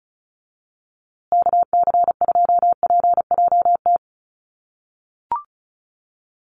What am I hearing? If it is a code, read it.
Morse code, heard as a keyed tone, about 34 wpm, 700 Hz. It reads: KC2PJT
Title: USA Call Signs Repeated 2x 35wpm